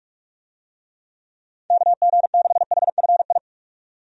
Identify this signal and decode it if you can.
Morse code, heard as a keyed tone, about 46 wpm, 700 Hz. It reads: KG6HFI
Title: USA Call Signs - 6 Characters - 45wpm